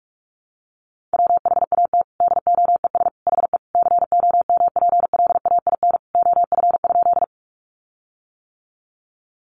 Morse code, heard as a keyed tone, about 44 wpm, 700 Hz. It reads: WHAT DOES HE COMPLAIN OF?
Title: Sentences from Top 1300 Words 45wpm